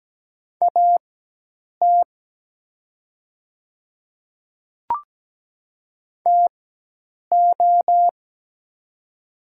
Morse code, heard as a keyed tone, about 17 wpm, spaced as for 8 wpm, 700 Hz. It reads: AT TO
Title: 4x Character Spacing - Top 200 Words - 17wpm